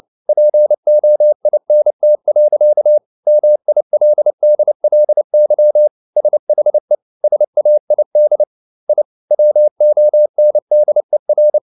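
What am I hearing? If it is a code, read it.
Morse code, heard as a keyed tone, about 29 wpm, 600 Hz. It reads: POINT. MILDLY SHE SAID I WONDER
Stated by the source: Morse code at 29 wpm, sending Ken Follett's novel Never in English, from CW med Gnister.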